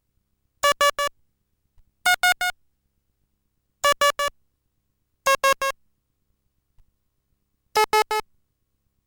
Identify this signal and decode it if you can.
There is no code.